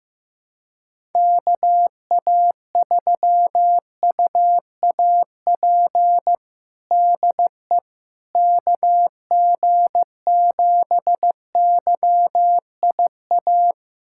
Morse code, heard as a keyed tone, about 15 wpm, 700 Hz. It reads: KA3UAP DE KG7YIA
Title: USA Callsign DE Callsign - 6 Characters - 15wpm